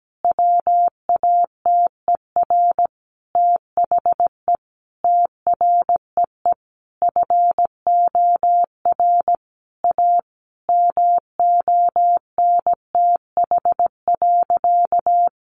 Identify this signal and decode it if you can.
Morse code, heard as a keyed tone, about 17 wpm, 700 Hz. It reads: WATER THE TREE FOR A MONTH.
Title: Easy Phrases 17wpm